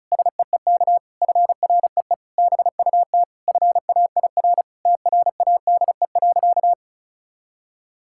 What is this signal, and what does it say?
Morse code, heard as a keyed tone, about 36 wpm, 700 Hz. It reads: SEEK FREE BUT FAIR TRADE.
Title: Sentences from Top 1600 Words 35wpm